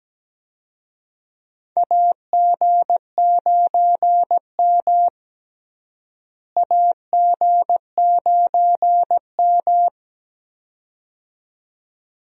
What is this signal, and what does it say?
Morse code, heard as a keyed tone, about 17 wpm, 700 Hz. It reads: AG9M AG9M
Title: USA Call Signs - 4 Characters - Repeated 2x - 17wpm